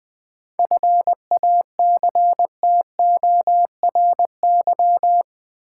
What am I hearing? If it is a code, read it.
Morse code, heard as a keyed tone, about 20 wpm, 700 Hz. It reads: FACTORY
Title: 7 Letter Words - First 3 Letters Given - 20wpm